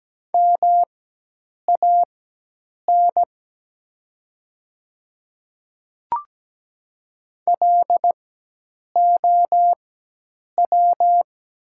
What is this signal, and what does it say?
Morse code, heard as a keyed tone, about 17 wpm, spaced as for 8 wpm, 700 Hz. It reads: MAN LOW
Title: 4x Character Spacing - Top 200 Words - 17wpm